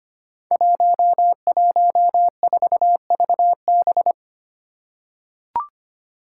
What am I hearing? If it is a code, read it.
Morse code, heard as a keyed tone, about 25 wpm, 700 Hz. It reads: 114VB
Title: Five Challenging Letter-Number Combinations 25wpm